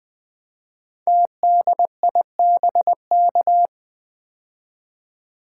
Morse code, heard as a keyed tone, about 20 wpm, 700 Hz. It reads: TDIBK
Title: Code Groups - 5 Letters - 20wpm